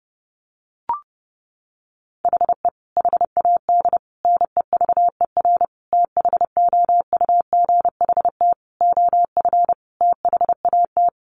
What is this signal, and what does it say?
Morse code, heard as a keyed tone, about 30 wpm, 700 Hz. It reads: HE HAD NEVER THOUGHT OF THAT
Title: Sentences from Top 1300 Words 30wpm